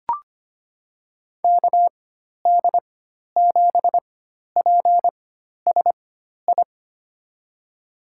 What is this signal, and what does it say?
Morse code, heard as a keyed tone, about 25 wpm, spaced as for 12 wpm, 700 Hz. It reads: KD7PSI
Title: USA Call Signs - 6 Characters - 4x Character Spacing - 25wpm